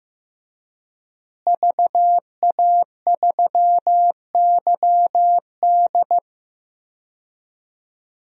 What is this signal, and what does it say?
Morse code, heard as a keyed tone, about 15 wpm, 700 Hz. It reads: VA3YD
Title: DXCC-100 Call Signs - 5 Characters - Repeated 2x - 15wpm